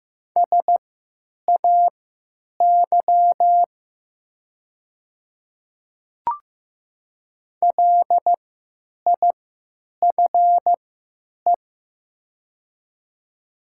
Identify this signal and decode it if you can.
Morse code, heard as a keyed tone, about 15 wpm, spaced as for 9 wpm, 700 Hz. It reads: SAY LIFE
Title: 3x Character Spacing - Top 300 Words - 15wpm